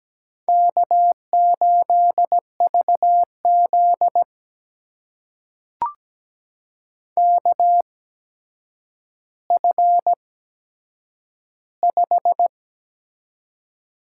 Morse code, heard as a keyed tone, about 17 wpm, spaced as for 8 wpm, 700 Hz. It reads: K8VZ KF5
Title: USA Call Signs - 8x Character Spacing - 17wpm